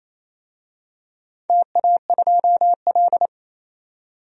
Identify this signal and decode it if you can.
Morse code, heard as a keyed tone, about 28 wpm, 700 Hz. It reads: TA2L
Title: DXCC-100 Call Signs - 4 Characters - 28wpm